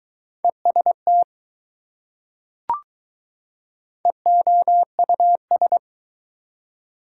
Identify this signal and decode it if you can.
Morse code, heard as a keyed tone, about 21 wpm, 700 Hz. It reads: EST EOUS